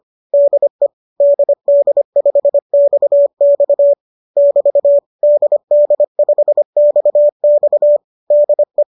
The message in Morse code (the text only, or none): DE DD5XX <BT> DD5XX DE